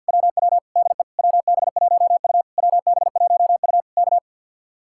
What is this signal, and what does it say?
Morse code, heard as a keyed tone, about 62 wpm, 700 Hz. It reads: WW DE WB1U WB1U K
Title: Speed-Racing Common QSO Phrases - 50wpm